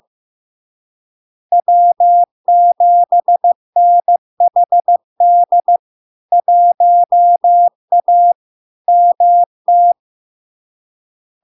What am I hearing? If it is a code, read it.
Morse code, heard as a keyed tone, about 15 wpm, 700 Hz. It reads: W7NHD 1A MT
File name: LARK-FD-2022-Log-in-CW-15wpm-700-Hz